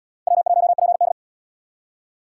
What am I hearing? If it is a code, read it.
Morse code, heard as a keyed tone, about 70 wpm, 700 Hz. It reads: W1PD